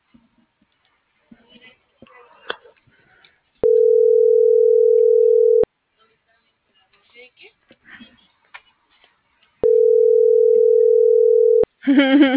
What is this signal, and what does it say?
Ringback tone (North America), 440 + 480 Hz.